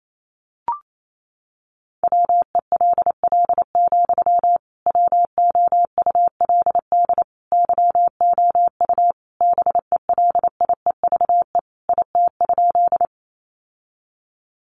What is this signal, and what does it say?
Morse code, heard as a keyed tone, about 28 wpm, 700 Hz. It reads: WELL, WOULD YOU BELIEVE IT?
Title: Sentences from Top 1000 Words 28wpm